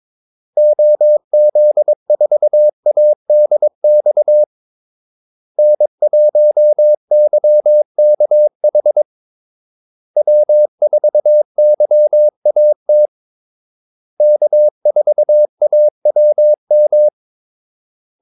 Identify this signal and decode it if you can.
Morse code, heard as a keyed tone, about 22 wpm, 600 Hz. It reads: OZ4ADX N1YKH W4YAT K4AWM
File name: Callsigns-22wpm